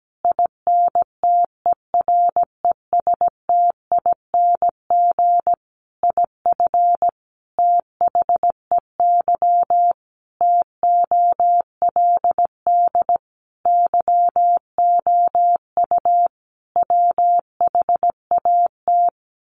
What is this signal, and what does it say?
Morse code, heard as a keyed tone, about 17 wpm, 700 Hz. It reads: INTERESTING IF THEY TOLD YOU WHAT
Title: Sentences from Top 1000 Words 17wpm